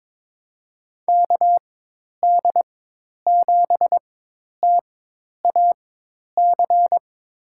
Morse code, heard as a keyed tone, about 22 wpm, spaced as for 10 wpm, 700 Hz. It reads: KD7TAC
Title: USA Call Signs - 6 Characters - 4x Character Spacing - 22wpm